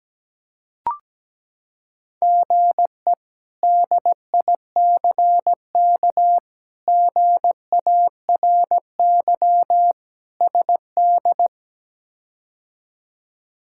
Morse code, heard as a keyed tone, about 17 wpm, 700 Hz. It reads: GE DICK GARY SD
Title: K1USN SST QSO Practice - 17wpm